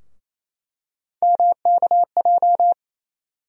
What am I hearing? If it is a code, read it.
Morse code, heard as a keyed tone, about 28 wpm, 700 Hz. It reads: MKJ
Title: USA Call Sign Postfixes - 3 Letters 28wpm